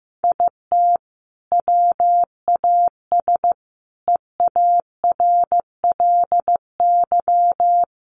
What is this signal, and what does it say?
Morse code, heard as a keyed tone, about 15 wpm, 700 Hz. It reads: IT WAS EARLY
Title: Sentences from Top 1500 Words 15wpm